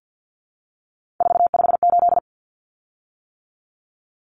Morse code, heard as a keyed tone, about 50 wpm, 700 Hz. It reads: 458
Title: Common Cut Numbers - 3-Digit 50wpm